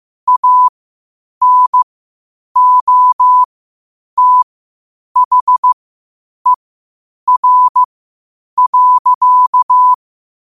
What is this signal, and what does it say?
Morse code, heard as a keyed tone, about 15 wpm, spaced as for 8 wpm, 1000 Hz. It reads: ANOTHER.
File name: qotdfiles / 8wpm